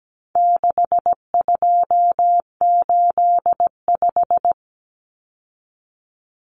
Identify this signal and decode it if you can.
Morse code, heard as a keyed tone, about 17 wpm, 700 Hz. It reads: 6285
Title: Common Cut Numbers - 4-Digit 17wpm